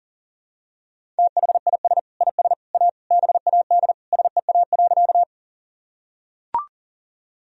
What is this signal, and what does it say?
Morse code, heard as a keyed tone, about 40 wpm, 700 Hz. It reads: THIS IS A BAD SEA.
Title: Sentences from Top 1000 Words - Review Part 1 - 40wpm